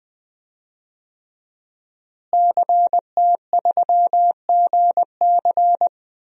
Morse code, heard as a keyed tone, about 20 wpm, 700 Hz. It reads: CT3GC
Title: DXCC-100 Call Signs - 5 Characters - Repeated 3x - 20wpm